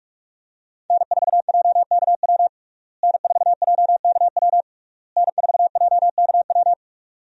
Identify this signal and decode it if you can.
Morse code, heard as a keyed tone, about 46 wpm, 700 Hz. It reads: N4JKW N4JKW N4JKW
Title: USA Call Signs - 5 Characters - Repeated 3x - 45wpm